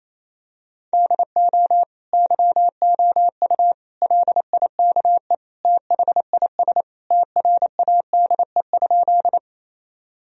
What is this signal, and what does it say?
Morse code, heard as a keyed tone, about 28 wpm, 700 Hz. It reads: DO YOU LIKE THIS TRADE?
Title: Sentences from Top 1000 Words - Review Part 4 - 28wpm